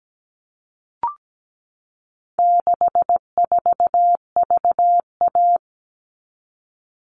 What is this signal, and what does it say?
Morse code, heard as a keyed tone, about 17 wpm, 700 Hz. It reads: 64VA
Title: All Cut Numbers - 4-Digit 17wpm